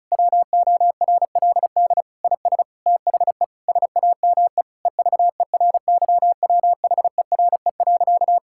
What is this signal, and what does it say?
Morse code, heard as a keyed tone, about 36 wpm, 700 Hz. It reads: WORLD IS THE SAME EVERYWHERE.
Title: Sentences from Top 1600 Words 35wpm